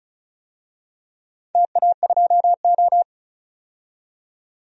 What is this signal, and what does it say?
Morse code, heard as a keyed tone, about 36 wpm, 700 Hz. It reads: TA2O
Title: DXCC-100 Call Signs - 4 Characters - 35wpm